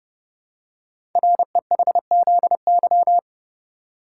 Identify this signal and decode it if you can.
Morse code, heard as a keyed tone, about 30 wpm, 700 Hz. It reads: REHZY
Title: Code Groups - 5 Letters - 30wpm